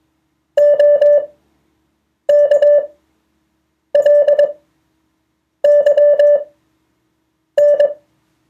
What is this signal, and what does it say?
Morse code, heard as a keyed tone, about 21 wpm, spaced as for 7 wpm, 580 Hz. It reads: OKLYN